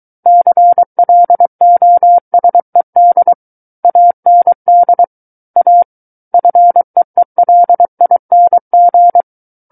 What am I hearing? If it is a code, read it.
Morse code, heard as a keyed tone, about 23 wpm, 700 Hz. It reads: CLOSED AND A FEELING
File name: Book One / 14-Chapter 14 - 23 WPM